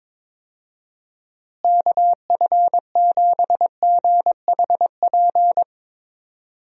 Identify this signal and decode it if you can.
Morse code, heard as a keyed tone, about 22 wpm, 700 Hz. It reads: KF7GHP
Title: USA Call Signs - 6 Characters - Repeated 2x - 22wpm